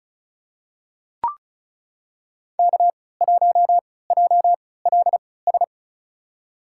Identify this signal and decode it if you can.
Morse code, heard as a keyed tone, about 36 wpm, spaced as for 20 wpm, 690 Hz. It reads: K1JLS
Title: USA Call Signs - 5 Characters - 3x Character Spacing - 35wpm